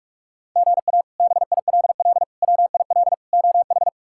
Morse code, heard as a keyed tone, about 44 wpm, 700 Hz. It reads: GA BILL WIL OH
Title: K1USN SST QSO Practice - 45wpm